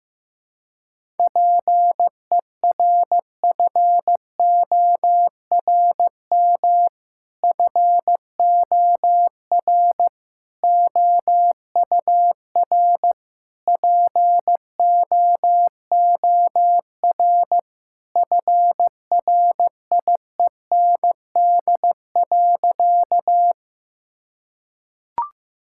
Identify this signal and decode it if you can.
Morse code, heard as a keyed tone, about 15 wpm, 700 Hz. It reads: PERFORM FOR OUR POOR FRIEND.
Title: Sentences from Top 1000 Words - Review Part 5 - 15wpm